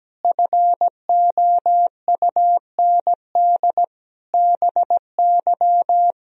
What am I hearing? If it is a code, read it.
Morse code, heard as a keyed tone, about 17 wpm, 700 Hz. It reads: FOUND BY
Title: Sentences from Top 1000 Words - Review Part 3 - 17wpm